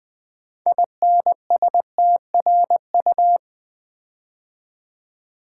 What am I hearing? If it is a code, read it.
Morse code, heard as a keyed tone, about 20 wpm, 700 Hz. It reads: INSTRU
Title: Build A Word - 6 Letters At A Time - 12-Letter Words - 20wpm